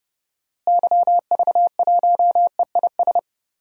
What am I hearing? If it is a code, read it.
Morse code, heard as a keyed tone, about 30 wpm, 700 Hz. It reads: YV1EIS
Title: DXCC-100 Call Signs - 6 Characters - Repeated 3x - 30wpm